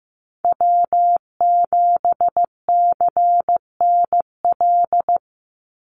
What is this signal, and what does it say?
Morse code, heard as a keyed tone, about 15 wpm, 700 Hz. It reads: W7CNL
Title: NA SOTA Chaser Callsigns 15wpm